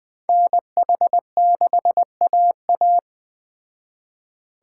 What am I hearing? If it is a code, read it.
Morse code, heard as a keyed tone, about 21 wpm, 700 Hz. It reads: NH6AA
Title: Speed Racing - USA 2x2 Call Signs 17wpm